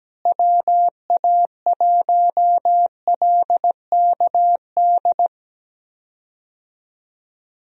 Morse code, heard as a keyed tone, about 17 wpm, 700 Hz. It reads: WA1LKD
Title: USA Call Signs - 6 Characters - Repeated 3x - 17wpm